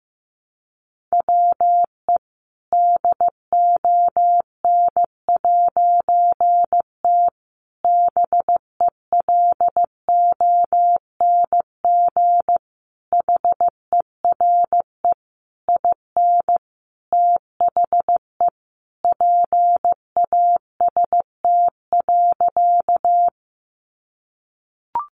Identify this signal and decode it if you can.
Morse code, heard as a keyed tone, about 15 wpm, 700 Hz. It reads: WE DON'T BELONG HERE IN THE PAST.
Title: A Sound Of Thunder 15wpm